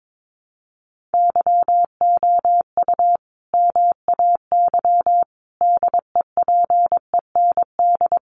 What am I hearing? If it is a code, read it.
Morse code, heard as a keyed tone, about 22 wpm, 700 Hz. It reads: YOU MAY DEPEND
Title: Sentences from Top 1300 Words 22wpm